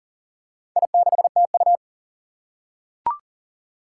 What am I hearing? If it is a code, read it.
Morse code, heard as a keyed tone, about 40 wpm, 700 Hz. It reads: I6TU
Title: Four Challenging Letter-Number Combinations 40wpm